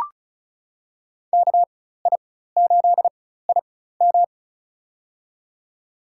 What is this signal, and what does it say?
Morse code, heard as a keyed tone, about 36 wpm, spaced as for 16 wpm, 700 Hz. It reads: KI8IM
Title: USA Call Signs - 5 Characters - 4x Character Spacing - 35wpm